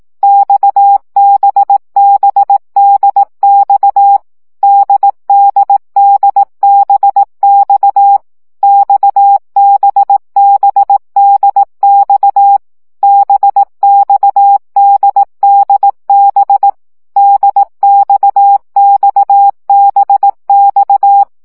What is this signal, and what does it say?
Morse code, heard as a keyed tone, about 18 wpm, 800 Hz. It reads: XBBDX DDDBX XBBDX BXDDB DXXBX
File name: CW gyakorlas d b x 15 baud